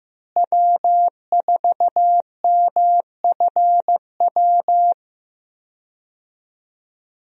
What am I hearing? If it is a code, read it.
Morse code, heard as a keyed tone, about 15 wpm, 700 Hz. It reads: W4MFW
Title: USA Call Signs - 5 Characters - Repeated 2x - 15wpm